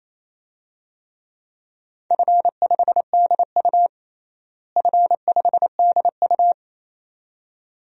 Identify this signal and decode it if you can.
Morse code, heard as a keyed tone, about 28 wpm, 700 Hz. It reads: F5DU F5DU
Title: DXCC-100 Call Signs - 4 Characters - Repeated 2x - 28wpm